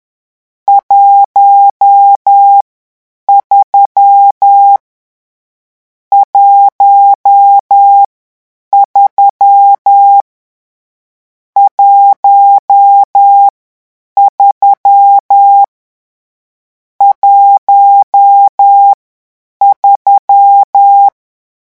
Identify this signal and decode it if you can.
Morse code, heard as a keyed tone, about 10 wpm, 800 Hz. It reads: 1 3 1 3 1 3 1 3
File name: morse mp3 / lecon10-vitesse 10